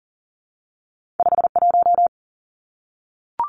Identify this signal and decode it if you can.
Morse code, heard as a keyed tone, about 40 wpm, 700 Hz. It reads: 5 1
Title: Two Challenging Letter-Number Combinations 40wpm